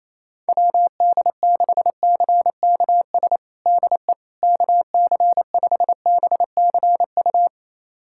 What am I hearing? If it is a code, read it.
Morse code, heard as a keyed tone, about 27 wpm, 700 Hz. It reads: WD6CKS DE KC5BCU
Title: USA Callsign DE Callsign - 6 Characters - 28wpm